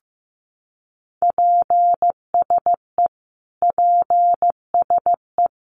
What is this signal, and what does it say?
Morse code, heard as a keyed tone, about 15 wpm, 700 Hz. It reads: PSE PSE
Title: Abbreviations Repeated 3x - 15wpm